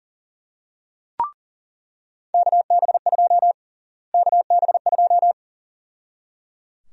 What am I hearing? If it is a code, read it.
Morse code, heard as a keyed tone, about 37 wpm, 700 Hz. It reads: KB2 KB2
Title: USA Call Sign Prefixes - Repeated 2x - 40wpm